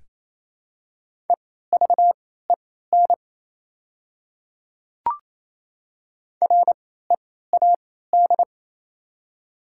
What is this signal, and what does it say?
Morse code, heard as a keyed tone, about 28 wpm, spaced as for 16 wpm, 700 Hz. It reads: EVEN READ
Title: 3x Character Spacing - Top 200 Words - 28wpm